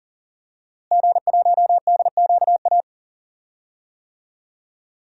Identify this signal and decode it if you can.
Morse code, heard as a keyed tone, about 38 wpm, 700 Hz. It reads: G1DQA
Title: DXCC-100 Call Signs - 5 Characters - Repeated 2x - 40wpm